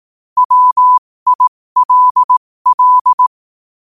Morse code, heard as a keyed tone, about 18 wpm, spaced as for 15 wpm, 1000 Hz. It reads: WILL
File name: qotdfiles / 15wpm